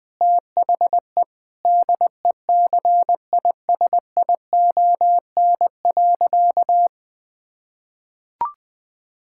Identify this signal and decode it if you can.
Morse code, heard as a keyed tone, about 20 wpm, 700 Hz. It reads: THE DECISION.
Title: Sentences with 6-Word Phrases - 20wpm